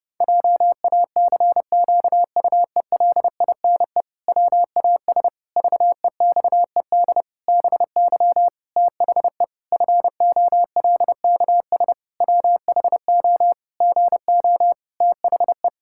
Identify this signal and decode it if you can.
Morse code, heard as a keyed tone, about 32 wpm, 700 Hz. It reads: JACQUELINE WAS VEXED BY THE FOLKS WHO GO THE
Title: Pangram Challenge v2 - 30wpm